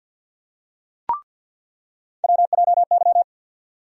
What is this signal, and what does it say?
Morse code, heard as a keyed tone, about 47 wpm, 700 Hz. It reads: WJY
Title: USA Call Sign Postfixes - 3 Letters 50wpm